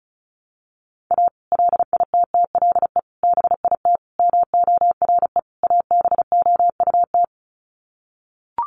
Morse code, heard as a keyed tone, about 34 wpm, 700 Hz. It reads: A LITTLE BIT MORE ABOUT
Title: Sets of 5 Words Encore2 35wpm